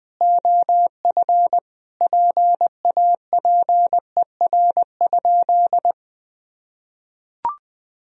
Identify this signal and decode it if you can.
Morse code, heard as a keyed tone, about 19 wpm, 700 Hz. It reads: OF PAPER?
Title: Sentences from Top 1000 Words - Review Part 2 - 20wpm